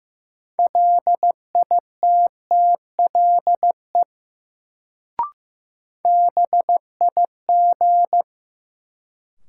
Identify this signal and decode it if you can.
Morse code, heard as a keyed tone, about 15 wpm, 700 Hz. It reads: LITTLE BIG
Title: Sentences from Top 200 Words - New Word Drill - 15wpm